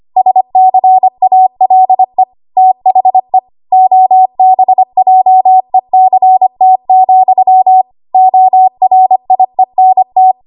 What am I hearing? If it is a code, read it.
Morse code, heard as a keyed tone, about 25 wpm, 750 Hz. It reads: SCALE THE OBJECT, ORIENT
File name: Archive / 25 WPM / 25WPM